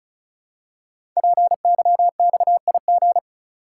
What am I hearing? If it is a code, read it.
Morse code, heard as a keyed tone, about 36 wpm, 700 Hz. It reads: PYXIG
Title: Code Groups - 5 Letters - 35wpm